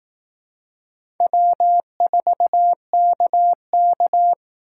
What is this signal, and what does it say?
Morse code, heard as a keyed tone, about 18 wpm, 700 Hz. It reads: W4KK
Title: Speed Racing - USA 1x2 Call Signs 15wpm